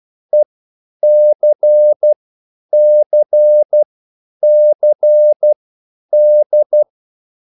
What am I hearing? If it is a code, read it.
Morse code, heard as a keyed tone, about 12 wpm, spaced as for 9 wpm, 600 Hz. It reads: ECCCD